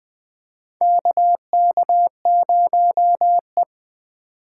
Morse code, heard as a keyed tone, about 20 wpm, 700 Hz. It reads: KK0E
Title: USA Call Signs - 4 Characters - 4x Character Spacing - 20wpm